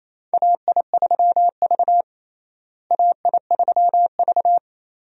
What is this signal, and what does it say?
Morse code, heard as a keyed tone, about 28 wpm, 700 Hz. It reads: AI3V AI3V
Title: USA Call Signs - 4 Characters - Repeated 2x - 28wpm